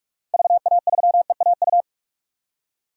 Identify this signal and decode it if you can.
Morse code, heard as a keyed tone, about 43 wpm, 700 Hz. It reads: VA3EAU
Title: DXCC-100 Call Signs - 6 Characters - 45wpm